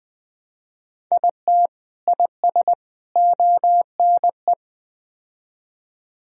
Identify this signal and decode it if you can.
Morse code, heard as a keyed tone, about 20 wpm, 700 Hz. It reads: IT IS ONE
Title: Build A Phrase - 5-Word Phrases - 20wpm